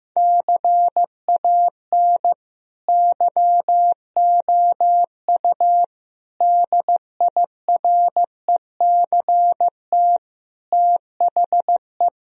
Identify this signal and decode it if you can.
Morse code, heard as a keyed tone, about 15 wpm, 700 Hz. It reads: CAN YOU DIRECT THE
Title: Sentences from Top 1000 Words - Review Part 2 - 15wpm